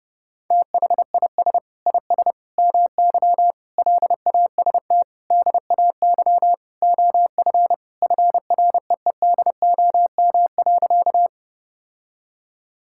Morse code, heard as a keyed tone, about 30 wpm, 700 Hz. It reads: THIS IS MY LAST DAY OF FREEDOM.
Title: Sentences from Top 1000 Words - Review Part 5 - 30wpm